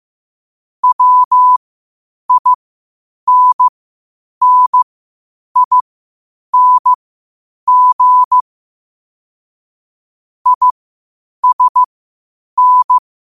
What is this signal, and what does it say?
Morse code, heard as a keyed tone, about 15 wpm, spaced as for 8 wpm, 1000 Hz. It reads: WINNING ISN
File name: qotdfiles / 8wpm